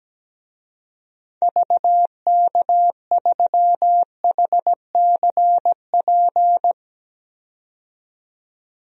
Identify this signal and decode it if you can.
Morse code, heard as a keyed tone, about 17 wpm, 700 Hz. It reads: VK3HCP
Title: DXCC-100 Call Signs - 6 Characters - 17wpm